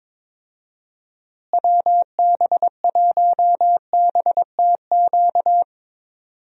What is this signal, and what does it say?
Morse code, heard as a keyed tone, about 22 wpm, 700 Hz. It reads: WB1BTQ
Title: USA Call Signs - 6 Characters - Repeated 3x - 22wpm